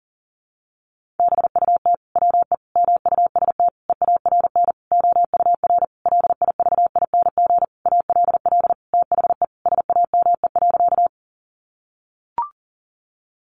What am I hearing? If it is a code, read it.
Morse code, heard as a keyed tone, about 40 wpm, 700 Hz. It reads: BUT WE MUST EARN OUR LIVING ALL THE SAME.
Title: Sentences from Top 1500 Words 40wpm